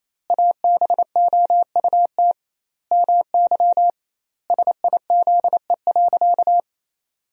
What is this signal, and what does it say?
Morse code, heard as a keyed tone, about 28 wpm, 700 Hz. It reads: ABOUT MY SIZE.
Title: 2x Word-Spacing - Sentences from Top 400 Words - 28wpm